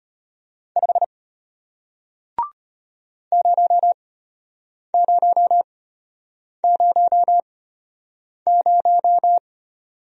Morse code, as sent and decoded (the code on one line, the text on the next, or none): ..... / ----- ----- ----- -----
5 0000